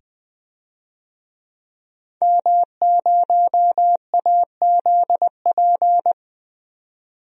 Morse code, heard as a keyed tone, about 20 wpm, 700 Hz. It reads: M0AZP
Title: DXCC-100 Call Signs - 5 Characters - 20wpm